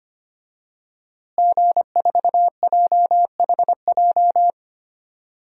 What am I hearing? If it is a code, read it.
Morse code, heard as a keyed tone, about 25 wpm, 700 Hz. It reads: G4JHJ
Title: DXCC-100 Call Signs - 5 Characters - Repeated 3x - 25wpm